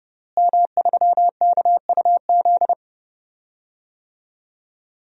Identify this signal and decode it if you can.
Morse code, heard as a keyed tone, about 30 wpm, 700 Hz. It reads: M3KUZ
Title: DXCC-100 Call Signs - 5 Characters - Repeated 2x - 30wpm